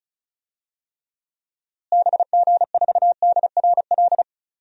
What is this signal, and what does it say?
Morse code, heard as a keyed tone, about 36 wpm, 700 Hz. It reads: BG4DRL
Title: DXCC-100 Call Signs - 6 Characters - Repeated 3x - 35wpm